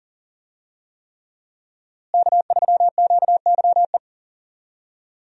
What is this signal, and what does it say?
Morse code, heard as a keyed tone, about 37 wpm, 700 Hz. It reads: K3QYE
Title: USA Call Signs - 5 Characters - Repeated 3x - 40wpm